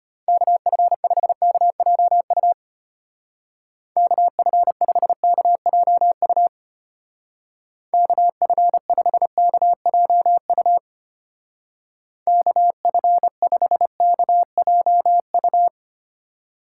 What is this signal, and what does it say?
Morse code, heard as a keyed tone, about 32 wpm, 700 Hz. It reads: KF5KJU KF5KJU KF5KJU KF5KJU